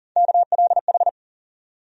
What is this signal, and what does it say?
Morse code, heard as a keyed tone, about 39 wpm, 700 Hz. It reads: KLH
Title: USA Call Sign Postfixes - 3 Letters 40wpm